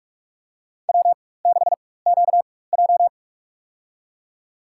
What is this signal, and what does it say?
Morse code, heard as a keyed tone, about 46 wpm, spaced as for 21 wpm, 690 Hz. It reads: W6QJ
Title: USA Call Signs - 4x Character Spacing - 45wpm